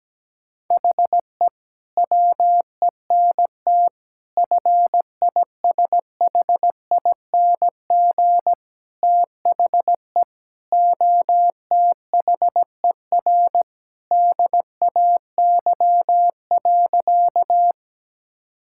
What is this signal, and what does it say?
Morse code, heard as a keyed tone, about 17 wpm, 700 Hz. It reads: HE WENT FISHING THE OTHER DAY.
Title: Sentences from Top 1000 Words - Review Part 5 - 17wpm